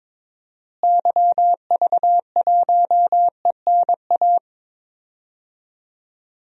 Morse code, heard as a keyed tone, about 21 wpm, 700 Hz. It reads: YV1ENA
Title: DXCC-100 Call Signs - 6 Characters - Repeated 2x - 22wpm